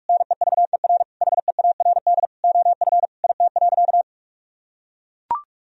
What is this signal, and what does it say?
Morse code, heard as a keyed tone, about 43 wpm, 700 Hz. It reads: NEVER HEARD OF IT.
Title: Sentences from Top 1000 Words - Review Part 3 - 45wpm